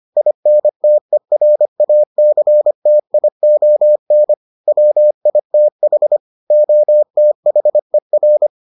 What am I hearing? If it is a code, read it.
Morse code, heard as a keyed tone, about 25 wpm, 600 Hz. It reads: INTERACTION WITH OTHER